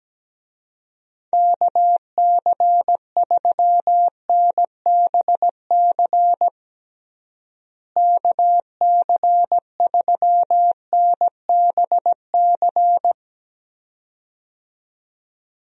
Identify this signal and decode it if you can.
Morse code, heard as a keyed tone, about 17 wpm, 700 Hz. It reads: KC3NBC KC3NBC
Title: USA Call Signs - 6 Characters - Repeated 2x - 17wpm